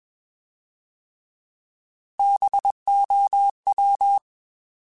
Morse code, heard as a keyed tone, about 21 wpm, 800 Hz. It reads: BOW
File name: Alphabetical / Alphabetical mp3s / Bs @20